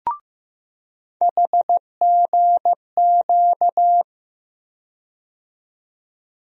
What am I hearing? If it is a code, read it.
Morse code, heard as a keyed tone, about 15 wpm, 700 Hz. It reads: HGQ
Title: USA Call Sign Postfixes - 3 Letters 15wpm